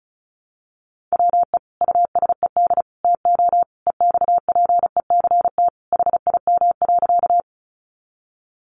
Morse code, heard as a keyed tone, about 34 wpm, 700 Hz. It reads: WE USED TO EXPECT HIM.
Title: Sentences from Top 1100 Words 35wpm